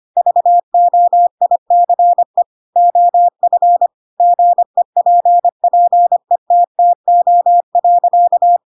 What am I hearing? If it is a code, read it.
Morse code, heard as a keyed tone, about 25 wpm, 700 Hz. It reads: VOICE OF GEPPETTO.